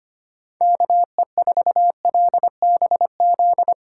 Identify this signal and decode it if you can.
Morse code, heard as a keyed tone, about 25 wpm, 700 Hz. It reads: KE4LBZ
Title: USA Call Signs - 6 Characters - 2x Character Spacing - 25wpm